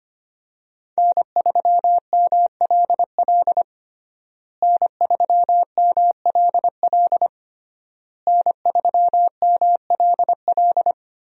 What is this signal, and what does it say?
Morse code, heard as a keyed tone, about 25 wpm, 700 Hz. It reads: N3MLL N3MLL N3MLL
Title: USA Call Signs - 5 Characters - Repeated 3x - 25wpm